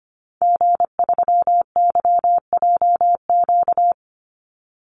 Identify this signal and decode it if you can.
Morse code, heard as a keyed tone, about 25 wpm, 700 Hz. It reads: G3YJQ
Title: Call Signs Repeated 3x - 25wpm